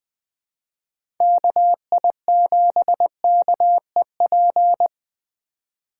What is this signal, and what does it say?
Morse code, heard as a keyed tone, about 20 wpm, 700 Hz. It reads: KI7KEP
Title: USA Call Signs - 4x Character Spacing - 20wpm